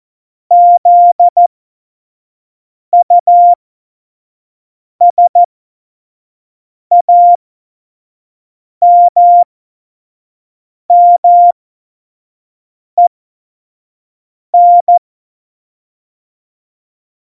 Morse code, heard as a keyed tone, about 14 wpm, spaced as for 5 wpm, 700 Hz. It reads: ZUSAMMEN
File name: Morsekurs / Witze / Witz 17. Oktober 17, Tempo 25